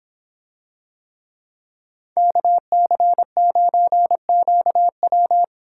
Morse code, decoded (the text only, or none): KC9QW